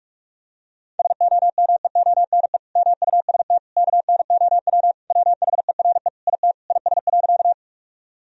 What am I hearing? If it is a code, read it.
Morse code, heard as a keyed tone, about 46 wpm, 700 Hz. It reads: SOMEONE MUST KNOW WHERE IT IS.
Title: Sentences from Top 1000 Words - Review Part 4 - 45wpm